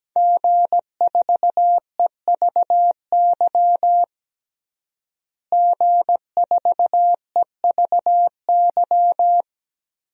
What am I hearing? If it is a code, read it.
Morse code, heard as a keyed tone, about 17 wpm, 700 Hz. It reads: G4EVY G4EVY
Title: DXCC-100 Call Signs - 5 Characters - Repeated 3x - 17wpm